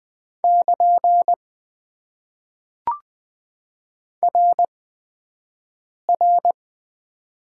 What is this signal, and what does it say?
Morse code, heard as a keyed tone, about 20 wpm, spaced as for 5 wpm, 700 Hz. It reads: <KN> RR
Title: Top 90 QSO Elements - 8x Character Spacing - 20wpm